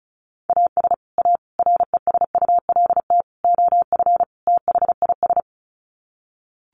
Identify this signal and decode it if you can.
Morse code, heard as a keyed tone, about 34 wpm, 700 Hz. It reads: AS A RESULT OF THIS
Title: Sets of 5 Words 35wpm